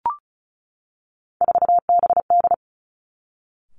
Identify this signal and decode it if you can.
Morse code, heard as a keyed tone, about 36 wpm, 700 Hz. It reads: 4BD
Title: Three Challenging Letter-Number Combinations 35wpm